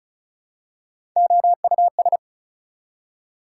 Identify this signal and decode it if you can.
Morse code, heard as a keyed tone, about 36 wpm, 700 Hz. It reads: OUS
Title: Prefixes and Suffixes 35wpm